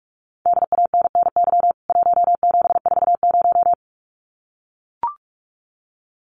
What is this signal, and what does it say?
Morse code, heard as a keyed tone, about 44 wpm, 700 Hz. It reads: DANNY 1740
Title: CWT Exchange 45wpm